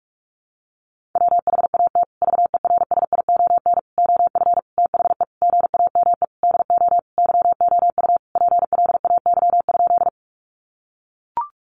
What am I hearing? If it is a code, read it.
Morse code, heard as a keyed tone, about 44 wpm, 700 Hz. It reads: WHAT VERSION OF THE GAME DO YOU PLAY?
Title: Sentences from Top 1400 Words 45wpm